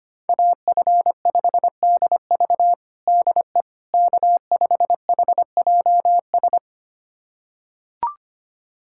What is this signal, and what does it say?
Morse code, heard as a keyed tone, about 25 wpm, 700 Hz. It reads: AF5DV DE K5HJS
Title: USA Callsign DE Callsign - 5 Characters - 25wpm